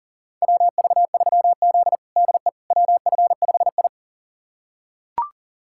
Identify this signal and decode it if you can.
Morse code, heard as a keyed tone, about 40 wpm, 700 Hz. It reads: WV3Z DE WF5I
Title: USA Callsign DE Callsign - 4 Characters - 40wpm